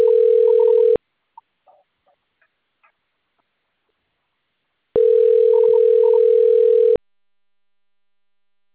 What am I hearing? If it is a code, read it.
Ringback tone (North America), 440 + 480 Hz.
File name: rg-1600-unknown- - - .37205